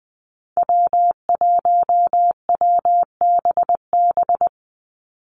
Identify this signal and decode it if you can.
Morse code, heard as a keyed tone, about 20 wpm, 700 Hz. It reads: W1WBB
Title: Call Signs Repeated 2x - 20wpm